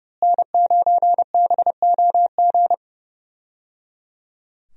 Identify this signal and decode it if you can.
Morse code, heard as a keyed tone, about 30 wpm, 700 Hz. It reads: N9BOG
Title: USA Call Signs - 5 Characters - Repeated 3x - 30wpm